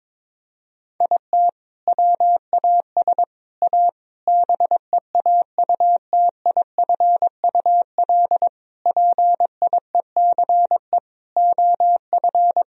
Morse code, heard as a keyed tone, about 22 wpm, 700 Hz. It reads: IT WAS A BEAUTIFUL PIECE OF
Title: Sentences from Top 1000 Words - Review Part 2 - 22wpm